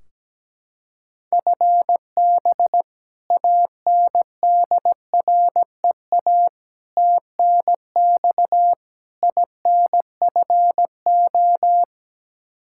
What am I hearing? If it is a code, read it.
Morse code, heard as a keyed tone, about 17 wpm, 700 Hz. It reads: FB ANDREA TNX INFO
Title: Practice QSO - Exchange 3 - More Info - 17wpm